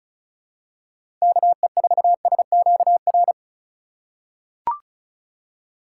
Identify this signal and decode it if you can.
Morse code, heard as a keyed tone, about 36 wpm, 700 Hz. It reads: KE4SQR
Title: USA Call Signs - 6 Characters - 4x Character Spacing - 35wpm